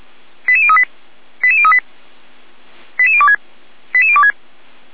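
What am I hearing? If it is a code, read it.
Five-tone selective call (ZVEI), heard as quick runs of five tones, tones of about 70 ms. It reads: 88038 88038 88026 88026
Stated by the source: ZVEI five-tone call received by SDR.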